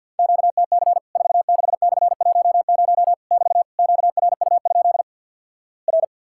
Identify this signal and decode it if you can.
Morse code, heard as a keyed tone, about 52 wpm, 700 Hz. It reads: QTC 46/10 <BT> QRU? R
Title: Worked All Europe DX Contest - QTC Exchange - 50wpm